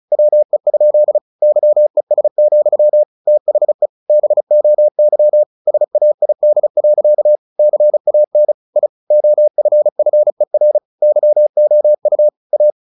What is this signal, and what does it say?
Morse code, heard as a keyed tone, about 34 wpm, 600 Hz. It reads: WE? YES, THE BOY SAID. CAN I OFFER YOU A
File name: the old man and the sea / the old man and the sea 1 35wpm